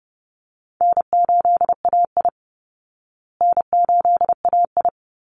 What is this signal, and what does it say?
Morse code, heard as a keyed tone, about 30 wpm, 700 Hz. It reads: N8AI N8AI
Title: Call Signs Repeated 3x - 30wpm